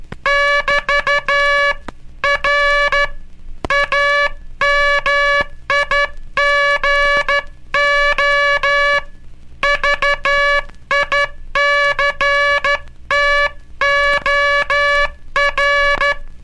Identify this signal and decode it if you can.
Morse code, heard as a keyed tone, about 12 wpm, 1700 Hz, with noise. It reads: <BT> R AMIGO VICTOR
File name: cursos / Curso CW mp3 / CW Aula - 18